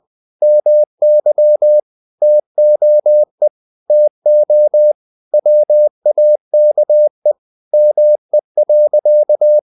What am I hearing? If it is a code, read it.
Morse code, heard as a keyed tone, about 20 wpm, 600 Hz. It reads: MY TOE TO WAKE ME.
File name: the old man and the sea / the old man and the sea part 1 20wpm